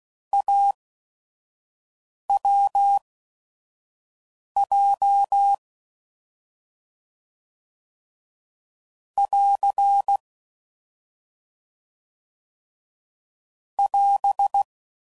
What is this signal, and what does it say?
Morse code, heard as a keyed tone, about 16 wpm, spaced as for 5 wpm, 800 Hz. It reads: AWJ <AR> <AS>